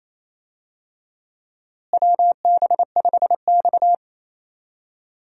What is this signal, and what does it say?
Morse code, heard as a keyed tone, about 28 wpm, 700 Hz. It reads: WB5X
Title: USA Call Signs - 4 Characters - Repeated 3x - 28wpm